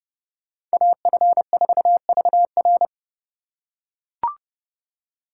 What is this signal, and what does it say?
Morse code, heard as a keyed tone, about 29 wpm, 700 Hz. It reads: AF4VR